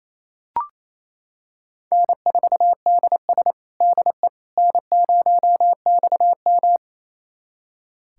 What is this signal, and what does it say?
Morse code, heard as a keyed tone, about 27 wpm, 700 Hz. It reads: N4DS DE N0XM
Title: USA Callsign DE Callsign - 4 Characters - 28wpm